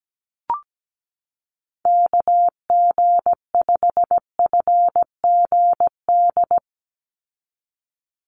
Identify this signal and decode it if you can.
Morse code, heard as a keyed tone, about 17 wpm, 700 Hz. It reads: KG5FGD
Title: USA Call Signs 17wpm